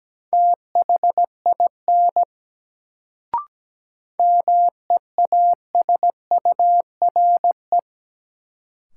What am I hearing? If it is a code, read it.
Morse code, heard as a keyed tone, about 17 wpm, 700 Hz. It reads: THIN MEASURE
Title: Sentences from Top 1200 Words - New Word Drill Alt - 17wpm